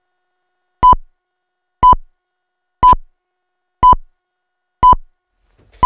Radio time signal (time pips). It is heard from 0.8 to 5.4 s.